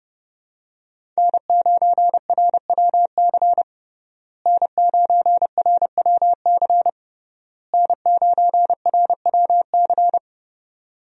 Morse code, heard as a keyed tone, about 30 wpm, 700 Hz. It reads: N9RWC N9RWC N9RWC
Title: USA Call Signs - 5 Characters - Repeated 3x - 30wpm